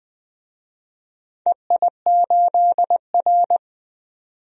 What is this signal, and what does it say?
Morse code, heard as a keyed tone, about 20 wpm, 700 Hz. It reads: EI8R
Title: DXCC-100 Call Signs - 4 Characters - 20wpm